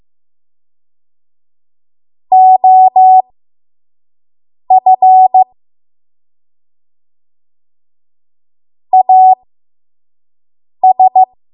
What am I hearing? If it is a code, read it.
Morse code, heard as a keyed tone, about 15 wpm, spaced as for 5 wpm, 750 Hz. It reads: OF AS
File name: Morse / 05WPM